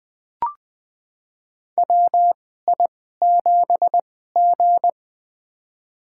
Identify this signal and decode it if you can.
Morse code, heard as a keyed tone, about 20 wpm, spaced as for 14 wpm, 700 Hz. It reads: WI7G
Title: USA Call Signs - 4 Characters - 2x Character Spacing - 20wpm